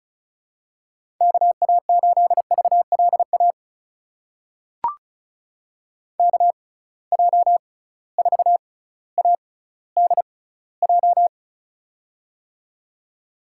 Morse code, heard as a keyed tone, about 34 wpm, spaced as for 18 wpm, 700 Hz. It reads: KA8VLA KJ4ADJ
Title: USA Call Signs - 6 Characters - 6x Character Spacing - 35wpm